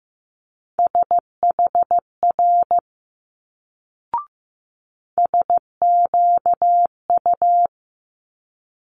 Morse code, heard as a keyed tone, about 15 wpm, 700 Hz. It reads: SHR SQU
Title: Multi-letter Phonemes Quick Fire 15wpm